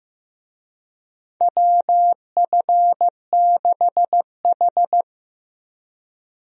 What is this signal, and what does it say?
Morse code, heard as a keyed tone, about 16 wpm, 700 Hz. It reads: WF6H
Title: USA Call Signs - 4 Characters - 15wpm